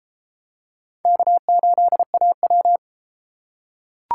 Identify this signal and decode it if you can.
Morse code, heard as a keyed tone, about 29 wpm, 700 Hz. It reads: K8AW